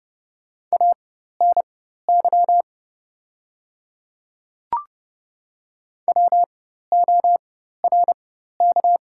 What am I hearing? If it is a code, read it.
Morse code, heard as a keyed tone, about 30 wpm, spaced as for 14 wpm, 700 Hz. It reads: ANY WORK
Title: 4x Character Spacing - Top 300 Words - 30wpm